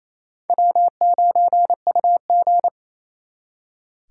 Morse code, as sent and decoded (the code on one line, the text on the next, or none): .-- ----. ..- --.
W9UG